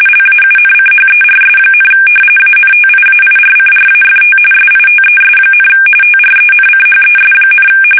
RTTY, about 45.45 baud. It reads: LAUNCHERS, ANTIAIRCRAFT SITES, TRENCHES,
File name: fsk850 / battleshipnj